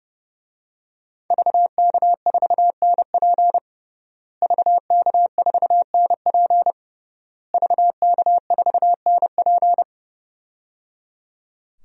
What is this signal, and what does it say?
Morse code, heard as a keyed tone, about 30 wpm, 700 Hz. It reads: VK4NP VK4NP VK4NP
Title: DXCC-100 Call Signs - 5 Characters - Repeated 3x - 30wpm